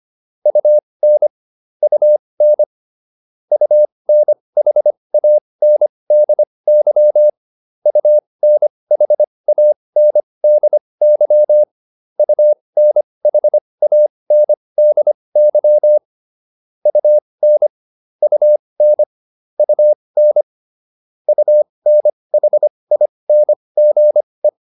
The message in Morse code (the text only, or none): UN UN UNHANDY UNHANDY UNHANDY UN UN UN UNHINGE